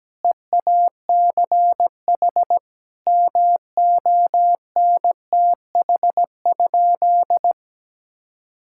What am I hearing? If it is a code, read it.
Morse code, heard as a keyed tone, about 17 wpm, 700 Hz. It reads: EACH MONTH?
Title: Sentences from Top 1000 Words - Review Part 6 - 17wpm